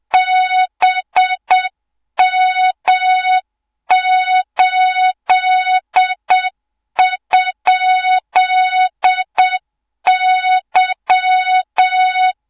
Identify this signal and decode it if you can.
Morse code, heard as a keyed tone, about 7 wpm, 3700 Hz, with noise. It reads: BM8?Y
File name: Morze-HA2MN / hang / 24 GYAKORLAT